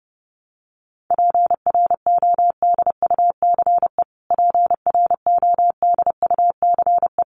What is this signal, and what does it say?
Morse code, heard as a keyed tone, about 30 wpm, 700 Hz. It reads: PRODUCE PRODUCE
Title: Top 500 Words Repeated 30wpm